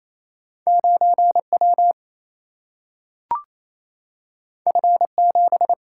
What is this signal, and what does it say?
Morse code, heard as a keyed tone, about 28 wpm, 700 Hz. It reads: 9 W F 7
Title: Single-Letter Single-Number 28wpm